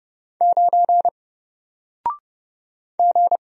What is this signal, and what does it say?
Morse code, heard as a keyed tone, about 31 wpm, 700 Hz. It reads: 9 G
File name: Single-Letter-Number-30wpm